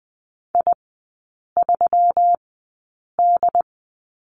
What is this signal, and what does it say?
Morse code, heard as a keyed tone, about 19 wpm, spaced as for 8 wpm, 700 Hz. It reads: I3D
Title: 133 Lesson 22e - Question Mark - Three Characters 20wpm